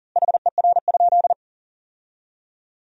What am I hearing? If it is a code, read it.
Morse code, heard as a keyed tone, about 40 wpm, 700 Hz. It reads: HER?
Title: 3x Word-Spacing - Sentences from Top 400 Words - 40wpm